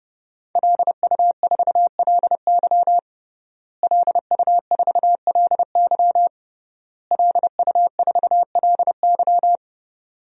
Morse code, heard as a keyed tone, about 32 wpm, 700 Hz. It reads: LU4LY LU4LY LU4LY
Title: DXCC-100 Call Signs - 5 Characters - Repeated 3x - 30wpm